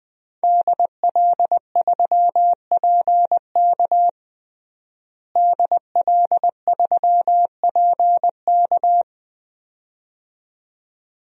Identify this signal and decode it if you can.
Morse code, heard as a keyed tone, about 21 wpm, 700 Hz. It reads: DL3PK DL3PK
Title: DXCC-100 Call Signs - 5 Characters - Repeated 2x - 20wpm